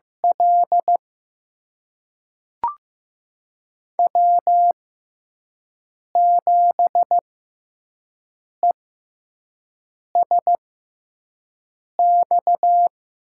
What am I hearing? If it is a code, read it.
Morse code, heard as a keyed tone, about 15 wpm, spaced as for 5 wpm, 700 Hz. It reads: L W7ESX
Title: USA Call Signs - 5 Characters - 6x Character Spacing - 15wpm